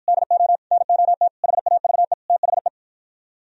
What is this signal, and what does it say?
Morse code, heard as a keyed tone, about 52 wpm, 700 Hz. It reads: DO NOT HAVE THE